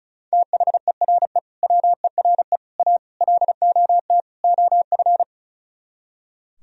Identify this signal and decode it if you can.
Morse code, heard as a keyed tone, about 42 wpm, 700 Hz. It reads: THERE WERE A LOT OF
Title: Speed-Racing - Sets of 5 Words 35wpm